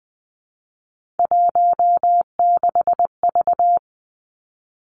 Morse code, heard as a keyed tone, about 20 wpm, 700 Hz. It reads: 16V
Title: Three Challenging Letter-Number Combinations 20wpm